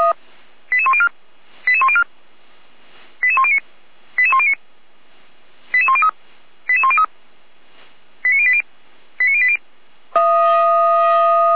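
Five-tone selective call (ZVEI), heard as quick runs of five tones, tones of about 70 ms. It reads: 88104 88104 88109 88109 88103 88103 89080 89080
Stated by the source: ZVEI five-tone selective call, received by SDR.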